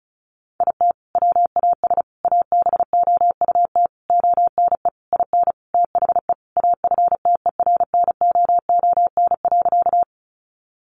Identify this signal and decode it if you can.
Morse code, heard as a keyed tone, about 34 wpm, 700 Hz. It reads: IT WAS ABOUT ONE IN THE AFTERNOON.
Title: Sentences from Top 1500 Words 35wpm